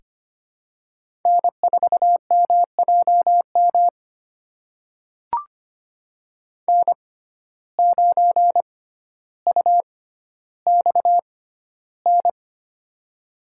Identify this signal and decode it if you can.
Morse code, heard as a keyed tone, about 25 wpm, spaced as for 13 wpm, 700 Hz. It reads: N4MJM N9UXN
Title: USA Call Signs - 6x Character Spacing - 25wpm